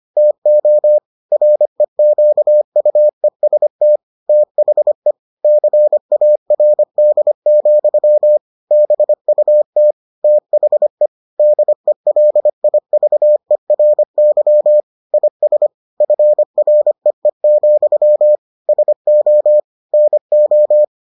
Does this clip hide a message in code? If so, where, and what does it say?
Yes: Morse code from 0.2 to 20.8 s, heard as a keyed tone, about 25 wpm, 600 Hz. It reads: TO REQUEST THE CARD, BUT THE DELIVERY IS FREE, SO NO